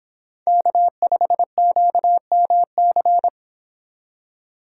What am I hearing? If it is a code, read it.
Morse code, heard as a keyed tone, about 31 wpm, 700 Hz. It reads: K5QMC